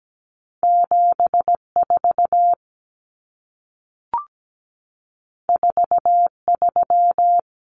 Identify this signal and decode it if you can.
Morse code, heard as a keyed tone, about 17 wpm, 700 Hz. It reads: 74 43
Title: Common Cut Numbers - 2-Digit 17wpm